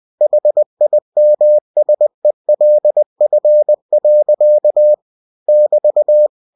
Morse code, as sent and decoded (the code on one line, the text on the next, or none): .... .. -- ... .-.. ..-. .-.-.- / -...-
HIMSELF. <BT>